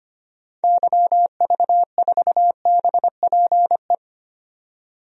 Morse code, heard as a keyed tone, about 25 wpm, 700 Hz. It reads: YV4BPE
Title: DXCC-100 Call Signs - 6 Characters - 25wpm